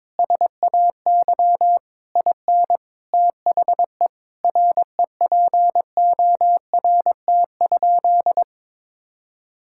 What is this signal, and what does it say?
Morse code, heard as a keyed tone, about 22 wpm, 700 Hz. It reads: SAY IN THE REPORT?
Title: Sentences from Top 1000 Words - Review Part 6 - 22wpm